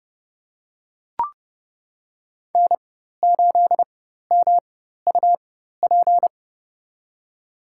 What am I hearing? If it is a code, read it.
Morse code, heard as a keyed tone, about 32 wpm, spaced as for 14 wpm, 700 Hz. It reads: N8MUP
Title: USA Call Signs - 5 Characters - 4x Character Spacing - 30wpm